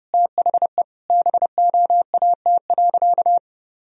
Morse code, heard as a keyed tone, about 30 wpm, 700 Hz. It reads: THE BOAT.
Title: Sentences from Top 1000 Words - Review Part 4 - 30wpm